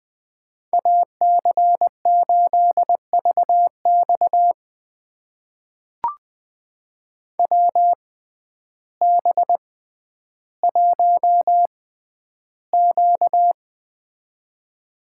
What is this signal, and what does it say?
Morse code, heard as a keyed tone, about 20 wpm, spaced as for 11 wpm, 700 Hz. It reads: AC8VX WB1Q
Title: USA Call Signs - 6x Character Spacing - 20wpm